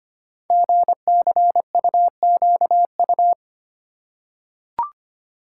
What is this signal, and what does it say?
Morse code, heard as a keyed tone, about 25 wpm, 700 Hz. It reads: GCUQU
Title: Code Groups - 5 Letters - 25wpm